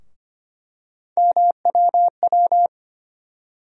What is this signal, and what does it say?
Morse code, heard as a keyed tone, about 25 wpm, 700 Hz. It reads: MWW
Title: USA Call Sign Postfixes - 3 Letters 25wpm